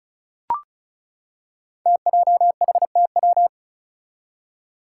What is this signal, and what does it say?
Morse code, heard as a keyed tone, about 36 wpm, 700 Hz. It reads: TJHTW
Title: Five Challenging Letter-Number Combinations 35wpm